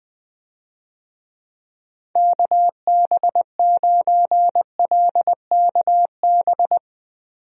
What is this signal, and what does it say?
Morse code, heard as a keyed tone, about 20 wpm, 700 Hz. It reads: KB9LKB
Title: USA Call Signs - 6 Characters - 20wpm